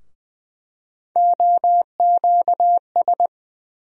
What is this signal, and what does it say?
Morse code, heard as a keyed tone, about 20 wpm, 700 Hz. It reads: OQS
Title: USA Call Sign Postfixes - 3 Letters 20wpm